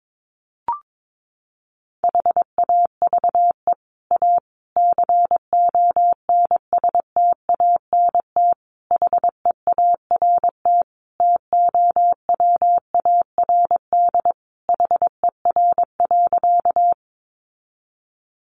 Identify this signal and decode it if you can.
Morse code, heard as a keyed tone, about 22 wpm, 700 Hz. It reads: HAVE A CONSTANT HEART TOWARD HER.
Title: Sentences from Top 1500 Words 22wpm